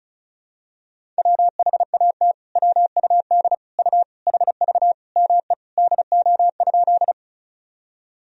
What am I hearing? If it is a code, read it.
Morse code, heard as a keyed tone, about 36 wpm, 700 Hz. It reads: WHAT WUD U HV ME DO?
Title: Sentences from Top 200 Words and Abbreviations 35wpm